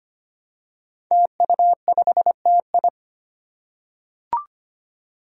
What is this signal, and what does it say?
Morse code, heard as a keyed tone, about 25 wpm, 700 Hz. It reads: TU5TI
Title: Five Challenging Letter-Number Combinations 25wpm